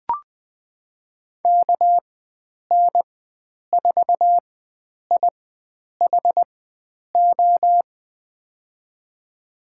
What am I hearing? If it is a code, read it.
Morse code, heard as a keyed tone, about 20 wpm, spaced as for 9 wpm, 700 Hz. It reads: KN4IHO
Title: USA Call Signs - 6 Characters - 4x Character Spacing - 20wpm